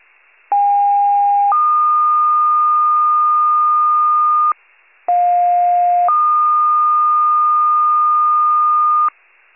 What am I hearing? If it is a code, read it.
Two-tone page, heard as a one-second tone, then a three-second tone, about 810 then 1220 Hz, about 710 then 1220 Hz. It is heard from 0.5 to 9.1 s.